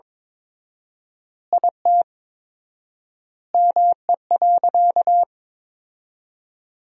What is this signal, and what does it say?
Morse code, heard as a keyed tone, about 22 wpm, 700 Hz. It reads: IT ME.
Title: 4x Word-Spacing - Sentences from Top 200 Words - 22wpm